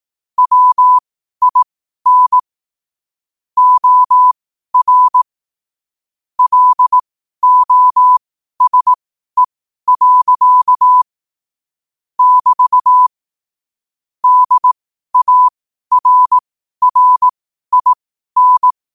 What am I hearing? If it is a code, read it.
Morse code, heard as a keyed tone, about 18 wpm, spaced as for 12 wpm, 1000 Hz. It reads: WIN OR LOSE. <BT> DARRIN
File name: qotdfiles / 12wpm